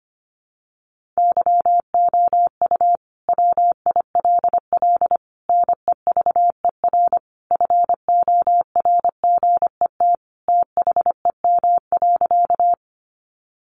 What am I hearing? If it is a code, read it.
Morse code, heard as a keyed tone, about 25 wpm, 700 Hz. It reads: YOU WILL NEVER FORGET THEM.
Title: Sentences from Top 1200 Words 25wpm